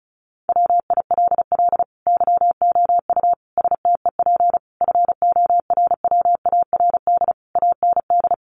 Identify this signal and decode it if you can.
Morse code, heard as a keyed tone, about 34 wpm, 700 Hz. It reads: WILL YOU STEP FORWARD AND
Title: Sentences from Top 1300 Words 35wpm